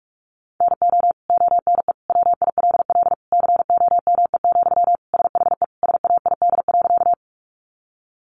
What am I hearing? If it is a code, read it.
Morse code, heard as a keyed tone, about 44 wpm, 700 Hz. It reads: NO ONE WILL COME, SHE SAID.
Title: Sentences from Top 100 Words 45wpm